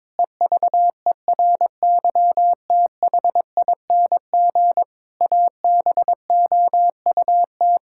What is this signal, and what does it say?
Morse code, heard as a keyed tone, about 22 wpm, 700 Hz. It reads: EVERYTHING ABOUT
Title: Pangram Challenge v2 - 22wpm